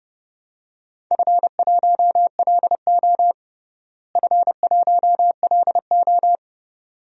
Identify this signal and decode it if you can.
Morse code, heard as a keyed tone, about 30 wpm, 700 Hz. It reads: F1LO F1LO
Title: DXCC-100 Call Signs - 4 Characters - Repeated 2x - 30wpm